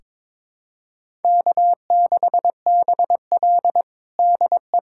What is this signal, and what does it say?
Morse code, heard as a keyed tone, about 23 wpm, 700 Hz. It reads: K6BL DE
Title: USA Callsign DE Callsign - 4 Characters - 22wpm